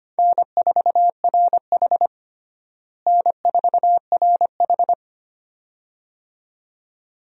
Morse code, heard as a keyed tone, about 25 wpm, 700 Hz. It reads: N4RH N4RH
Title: USA Call Signs - 4 Characters - Repeated 2x - 25wpm